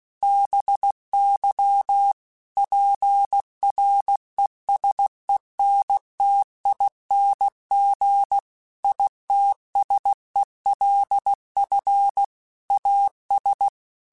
Morse code, heard as a keyed tone, about 16 wpm, 800 Hz. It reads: BY PRESENTING ITSELF AS
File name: Technical Subjects / Technical Subjects-mp3s / XC @15